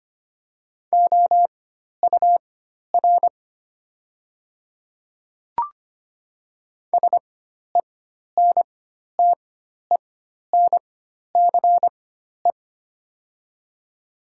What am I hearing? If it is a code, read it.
Morse code, heard as a keyed tone, about 25 wpm, spaced as for 12 wpm, 700 Hz. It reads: OUR SENTENCE
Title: 4x Character Spacing - Top 200 Words - 25wpm